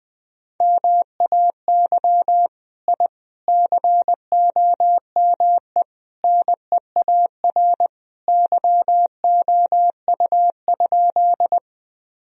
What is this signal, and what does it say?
Morse code, heard as a keyed tone, about 20 wpm, 700 Hz. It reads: MAY I COME NEAR YOU?
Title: Sentences from Top 1000 Words - Review Part 1 - 20wpm